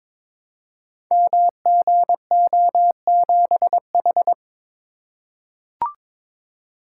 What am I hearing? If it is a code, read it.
Morse code, heard as a keyed tone, about 22 wpm, 700 Hz. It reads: MGO7H
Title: Code Groups - 5 Letter:Numbers - 22wpm